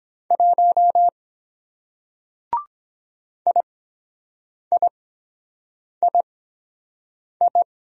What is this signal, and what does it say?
Morse code, heard as a keyed tone, about 23 wpm, spaced as for 7 wpm, 700 Hz. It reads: 1 IIII